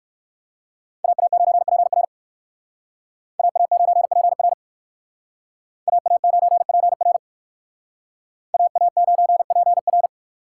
Morse code, heard as a keyed tone, about 59 wpm, 700 Hz. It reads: AA9PR AA9PR AA9PR AA9PR